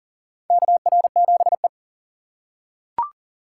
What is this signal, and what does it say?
Morse code, heard as a keyed tone, about 40 wpm, 700 Hz. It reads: KR7E
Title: USA Call Signs - 4 Characters - 3x Character Spacing - 40wpm